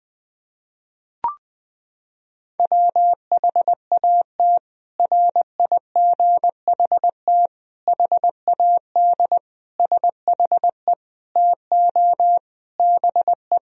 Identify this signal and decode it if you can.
Morse code, heard as a keyed tone, about 21 wpm, 700 Hz. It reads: WHAT RIGHT HAD SHE TO BE
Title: Sentences from Top 1000 Words - Review Part 1 - 20wpm